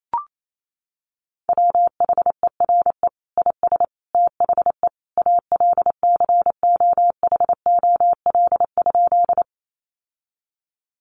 Morse code, heard as a keyed tone, about 28 wpm, 700 Hz. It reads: WHERE IS THE ALCOHOL?
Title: Sentences from Top 1400 Words 28wpm